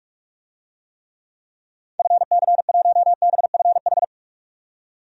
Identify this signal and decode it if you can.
Morse code, heard as a keyed tone, about 41 wpm, 700 Hz. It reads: FC1BFH
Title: DXCC-100 Call Signs - 6 Characters - 45wpm